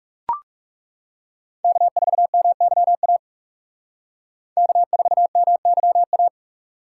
Morse code, heard as a keyed tone, about 38 wpm, 700 Hz. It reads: K4MYA K4MYA